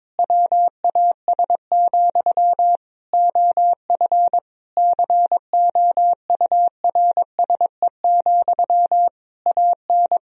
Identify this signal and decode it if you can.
Morse code, heard as a keyed tone, about 21 wpm, 700 Hz. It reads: WAS, OF COURSE, AN
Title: Sentences from Top 1000 Words - Review Part 3 - 22wpm